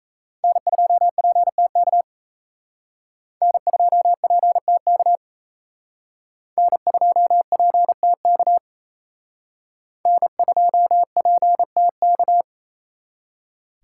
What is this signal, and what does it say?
Morse code, heard as a keyed tone, about 35 wpm, 700 Hz. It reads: N2PTK N2PTK N2PTK N2PTK
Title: Speed Racing - USA 1x3 Call Signs 28wpm